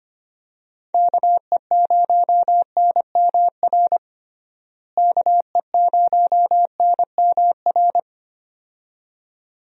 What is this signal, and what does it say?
Morse code, heard as a keyed tone, about 25 wpm, 700 Hz. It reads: KE0NMR KE0NMR
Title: USA Call Signs - 6 Characters - Repeated 2x - 25wpm